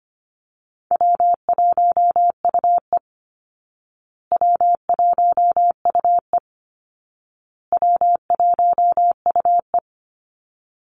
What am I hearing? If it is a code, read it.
Morse code, heard as a keyed tone, about 25 wpm, 700 Hz. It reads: W1UE W1UE W1UE
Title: Easy Call Signs Repeated 3x - 25wpm